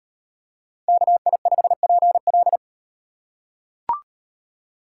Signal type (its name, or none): Morse code